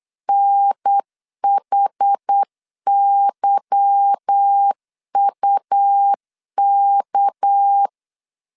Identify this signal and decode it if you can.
Morse code, heard as a keyed tone, about 8 wpm, 800 Hz. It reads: NHYUK
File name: LEZIONI / 8 YJ45 / 8 YJ 8wpm 3313